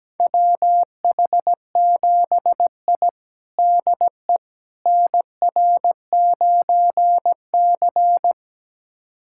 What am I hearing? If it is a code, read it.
Morse code, heard as a keyed tone, about 17 wpm, 700 Hz. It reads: WH7I DE NR9C
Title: USA Callsign DE Callsign - 4 Characters - 17wpm